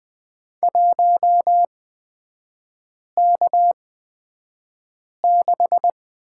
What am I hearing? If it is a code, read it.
Morse code, heard as a keyed tone, about 20 wpm, spaced as for 5 wpm, 690 Hz. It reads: 1K6
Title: 229 Lesson 37e - Letter x - Three Characters 20wpm